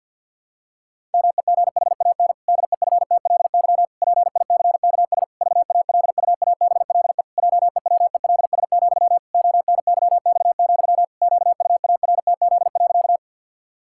Morse code, heard as a keyed tone, about 51 wpm, 700 Hz. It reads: MEGHAN DEFTLY PICKS VALUABLE JEWELS, ONYX, QUARTZ.
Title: Pangram Challenge v2 - 50wpm